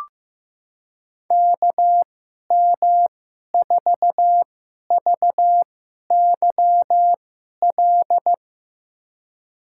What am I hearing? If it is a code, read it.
Morse code, heard as a keyed tone, about 15 wpm, spaced as for 11 wpm, 700 Hz. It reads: KM4VYL
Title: USA Call Signs - 6 Characters - 2x Character Spacing - 15wpm